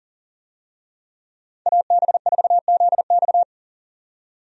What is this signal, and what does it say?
Morse code, heard as a keyed tone, about 37 wpm, 700 Hz. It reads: AB4ZX
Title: USA Call Signs - 5 Characters - 40wpm